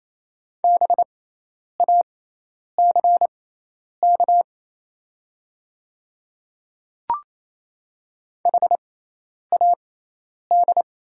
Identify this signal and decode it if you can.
Morse code, heard as a keyed tone, about 28 wpm, spaced as for 10 wpm, 690 Hz. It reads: BACK HAD
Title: 6x Character Spacing - Top 200 Words - 28wpm